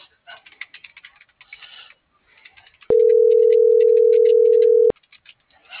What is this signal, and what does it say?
Ringback tone (North America), 440 + 480 Hz.